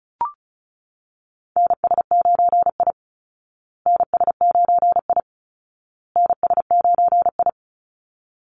Morse code, heard as a keyed tone, about 34 wpm, 700 Hz. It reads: NS9I NS9I NS9I
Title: Call Signs Repeated 2x - 35wpm